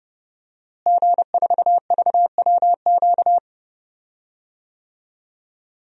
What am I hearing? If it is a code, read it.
Morse code, heard as a keyed tone, about 32 wpm, 700 Hz. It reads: G4VWQ
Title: DXCC-100 Call Signs - 5 Characters - Repeated 2x - 30wpm